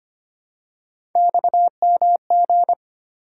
Morse code, heard as a keyed tone, about 25 wpm, 700 Hz. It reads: XMG
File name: Three Letter-Number Combinations - 25wpm